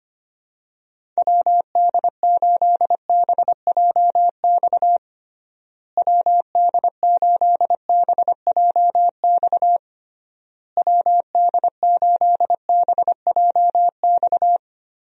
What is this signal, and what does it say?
Morse code, heard as a keyed tone, about 25 wpm, 700 Hz. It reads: WD8BJX WD8BJX WD8BJX
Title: USA Call Signs - 6 Characters - Repeated 3x - 25wpm